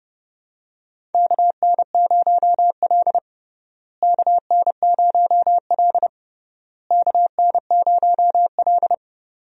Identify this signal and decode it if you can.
Morse code, heard as a keyed tone, about 32 wpm, 700 Hz. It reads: KN0L KN0L KN0L
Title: USA Call Signs - 4 Characters - Repeated 3x - 30wpm